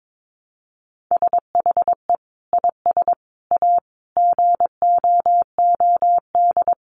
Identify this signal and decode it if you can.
Morse code, heard as a keyed tone, about 22 wpm, 700 Hz. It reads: SHE IS A GOOD
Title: Sentences from Top 1100 Words 22wpm